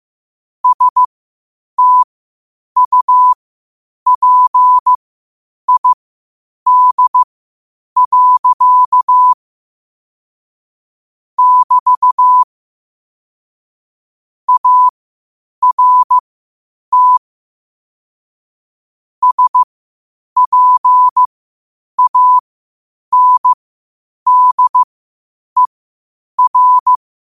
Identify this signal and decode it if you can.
Morse code, heard as a keyed tone, about 15 wpm, spaced as for 8 wpm, 1000 Hz. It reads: STUPID. <BT> ART SPANDER
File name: qotdfiles / 8wpm